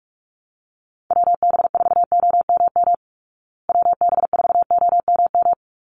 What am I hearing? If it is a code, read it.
Morse code, heard as a keyed tone, about 44 wpm, 700 Hz. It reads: WB4OMM WB4OMM
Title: Call Signs Repeated 3x - 45wpm